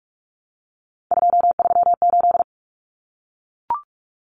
Morse code, heard as a keyed tone, about 44 wpm, 700 Hz. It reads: 238
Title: Numbers - 3-Digit 45wpm